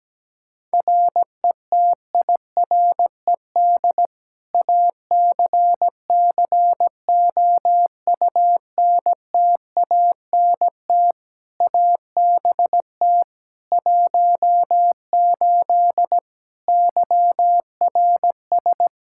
Morse code, heard as a keyed tone, about 17 wpm, 700 Hz. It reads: RETIRED ACCOUNTANT ABT 18 YRS
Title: K1USN SST QSO Practice 17wpm